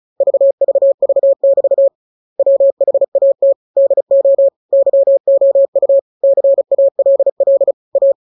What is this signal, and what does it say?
Morse code, heard as a keyed tone, about 34 wpm, 550 Hz. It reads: VVV<BT> WHAT DO YOU CALL A